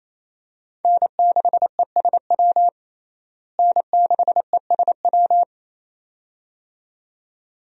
Morse code, heard as a keyed tone, about 28 wpm, 700 Hz. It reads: N6ESW N6ESW
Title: USA Call Signs - 5 Characters - Repeated 2x - 28wpm